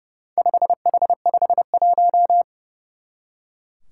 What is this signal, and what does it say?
Morse code, heard as a keyed tone, about 30 wpm, 700 Hz. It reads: 5H51
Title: Four Challenging Letter-Number Combinations 30wpm